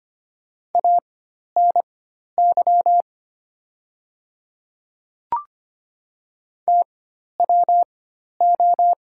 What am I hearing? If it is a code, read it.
Morse code, heard as a keyed tone, about 25 wpm, spaced as for 12 wpm, 700 Hz. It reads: ANY TWO
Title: 4x Character Spacing - Top 500 Words - 25wpm